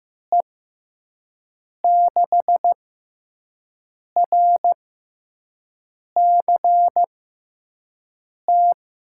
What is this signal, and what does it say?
Morse code, heard as a keyed tone, about 15 wpm, spaced as for 5 wpm, 700 Hz. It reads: E6RCT
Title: USA Call Signs - 6 Characters - 6x Character Spacing - 15wpm